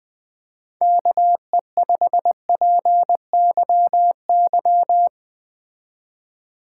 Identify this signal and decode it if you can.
Morse code, heard as a keyed tone, about 20 wpm, 700 Hz. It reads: KE5PYY
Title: Speed Racing - USA 2x3 Call Signs 17wpm